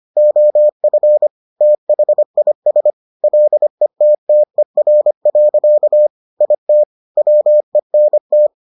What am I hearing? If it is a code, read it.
Morse code, heard as a keyed tone, about 25 wpm, 600 Hz. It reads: OF THIS LETTER. IT WENT